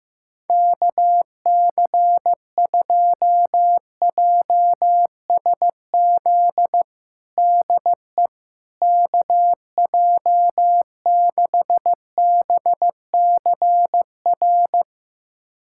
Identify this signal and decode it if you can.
Morse code, heard as a keyed tone, about 16 wpm, 700 Hz. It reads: KC2JSZ DE KJ6BCR
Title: USA Callsign DE Callsign - 6 Characters - 15wpm